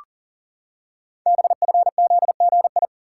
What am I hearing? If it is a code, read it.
Morse code, heard as a keyed tone, about 43 wpm, 700 Hz. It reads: BFZGI
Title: Code Groups - 5 Letters - 40wpm